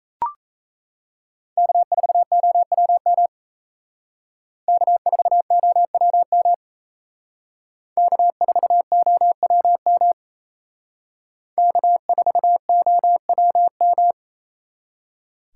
Morse code, heard as a keyed tone, about 35 wpm, 700 Hz. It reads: K4OWM K4OWM K4OWM K4OWM